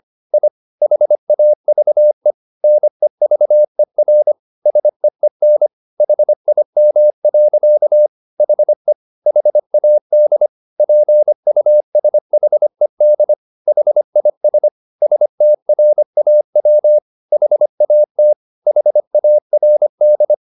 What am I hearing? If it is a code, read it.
Morse code, heard as a keyed tone, about 25 wpm, 600 Hz. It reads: I HAVE NEVER SEEN HIM. HE HAD PUSHED HIS STRAW HAT HARD
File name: the old man and the sea part 1 25wpm